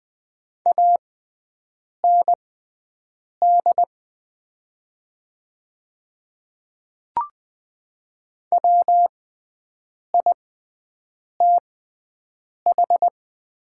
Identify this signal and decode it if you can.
Morse code, heard as a keyed tone, about 20 wpm, spaced as for 7 wpm, 700 Hz. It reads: AND WITH
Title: 6x Character Spacing - Top 100 Words - 20wpm